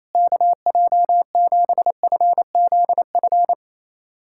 Morse code, heard as a keyed tone, about 28 wpm, 700 Hz. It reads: KJ7FZF
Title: USA Callsign DE Callsign 28wpm